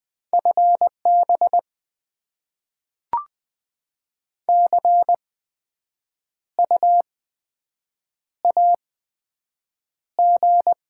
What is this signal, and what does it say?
Morse code, heard as a keyed tone, about 20 wpm, spaced as for 7 wpm, 700 Hz. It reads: FB CU AG
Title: Top 20 QSO Elements - 8x Character Spacing - 20wpm